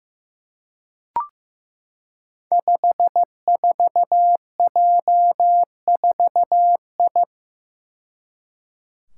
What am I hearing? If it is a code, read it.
Morse code, heard as a keyed tone, about 15 wpm, 700 Hz. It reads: 54J4I
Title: Five Challenging Letter-Number Combinations 15wpm